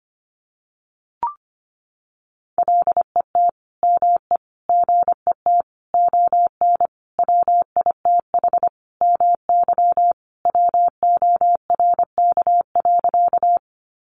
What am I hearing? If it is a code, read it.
Morse code, heard as a keyed tone, about 25 wpm, 700 Hz. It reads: LET ME GET ON WITH MY WORK.
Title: Sentences from Top 300 Words 25wpm